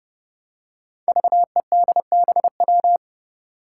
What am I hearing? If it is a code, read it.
Morse code, heard as a keyed tone, about 30 wpm, 700 Hz. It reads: VEDBW
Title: Five Challenging Letter-Number Combinations 30wpm